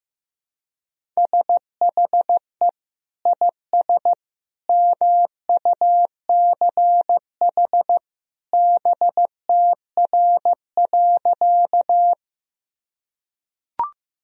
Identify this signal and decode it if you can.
Morse code, heard as a keyed tone, about 15 wpm, 700 Hz. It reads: SHE IS MUCH BTR.
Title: Sentences from Top 400 Words and Abbreviations 15wpm